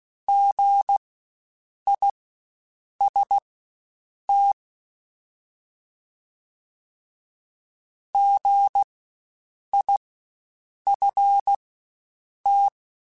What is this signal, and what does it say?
Morse code, heard as a keyed tone, about 16 wpm, spaced as for 7 wpm, 800 Hz. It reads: GIST GIFT
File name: Spaced Alphas / Spaced Alphas mp3s / Gs sp @15